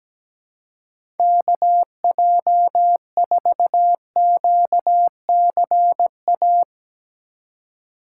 Morse code, heard as a keyed tone, about 17 wpm, 700 Hz. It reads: KJ4QCA
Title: USA Call Signs - 4x Character Spacing - 17wpm